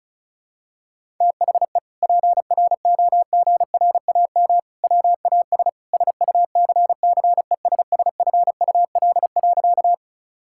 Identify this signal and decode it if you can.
Morse code, heard as a keyed tone, about 36 wpm, 700 Hz. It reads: THE PROGRAM WAS SUCCESSFUL.
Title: Sentences from Top 1000 Words - Review Part 5 - 35wpm